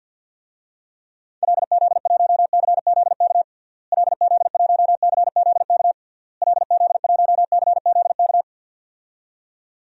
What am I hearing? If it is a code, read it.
Morse code, heard as a keyed tone, about 50 wpm, 700 Hz. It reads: LZ1CZK LZ1CZK LZ1CZK
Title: DXCC-100 Call Signs - 6 Characters - Repeated 3x - 50wpm